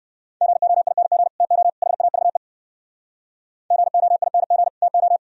Morse code, heard as a keyed tone, about 60 wpm, 700 Hz. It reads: GOING TO HAVE GOING TO